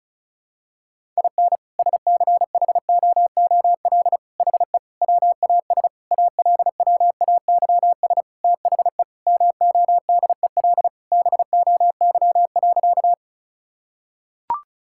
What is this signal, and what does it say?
Morse code, heard as a keyed tone, about 36 wpm, 700 Hz. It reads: IN SCHOOL HE WAS ALWAYS THE MODEL BOY.
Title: Sentences from Top 1000 Words - Review Part 5 - 35wpm